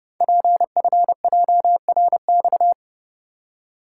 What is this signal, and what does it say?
Morse code, heard as a keyed tone, about 30 wpm, 700 Hz. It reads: PFJRX
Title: Code Groups - 5 Letters - 30wpm